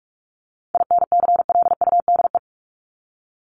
Morse code, heard as a keyed tone, about 44 wpm, 700 Hz. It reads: INCLUDE
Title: ING Suffix - Top 100 Words with Base Words 45wpm